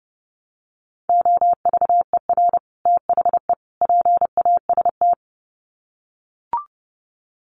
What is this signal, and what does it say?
Morse code, heard as a keyed tone, about 30 wpm, 700 Hz. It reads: OVER THE PAST
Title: Sets of 3 Words 30wpm